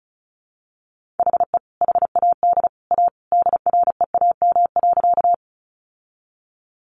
Morse code, heard as a keyed tone, about 34 wpm, 700 Hz. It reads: HE HAD A DREAM.
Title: Sentences from Top 1500 Words 35wpm